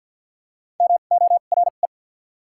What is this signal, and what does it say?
Morse code, heard as a keyed tone, about 54 wpm, spaced as for 37 wpm, 690 Hz. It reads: MORE
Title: 2x Character Spacing - Top 100 Words - 50wpm